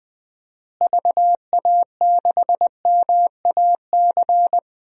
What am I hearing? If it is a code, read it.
Morse code, heard as a keyed tone, about 20 wpm, 700 Hz. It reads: VA6MAC
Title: DXCC-100 Call Signs - 6 Characters - Repeated 3x - 20wpm